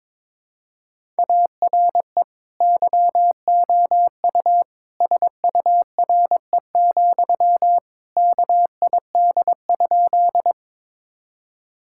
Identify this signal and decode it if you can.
Morse code, heard as a keyed tone, about 22 wpm, 700 Hz. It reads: ARE YOU SURE, KID?
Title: Sentences from Top 1600 Words 22wpm